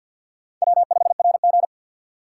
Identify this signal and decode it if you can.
Morse code, heard as a keyed tone, about 52 wpm, 700 Hz. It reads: W5RG
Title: USA Call Signs - 4 Characters - 2x Character Spacing - 50wpm